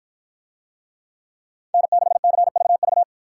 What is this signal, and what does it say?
Morse code, heard as a keyed tone, about 52 wpm, 700 Hz. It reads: N6CVV